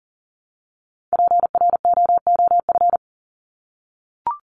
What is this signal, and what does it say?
Morse code, heard as a keyed tone, about 38 wpm, 700 Hz. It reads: PROOF
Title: Top 1501-1750 Words 40wpm